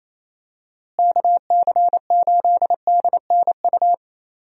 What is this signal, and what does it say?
Morse code, heard as a keyed tone, about 28 wpm, 700 Hz. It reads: KC8DNU
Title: USA Call Signs - 6 Characters - Repeated 3x - 28wpm